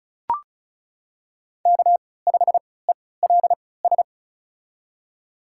Morse code, heard as a keyed tone, about 36 wpm, spaced as for 20 wpm, 700 Hz. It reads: K5ELS
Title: USA Call Signs - 5 Characters - 3x Character Spacing - 35wpm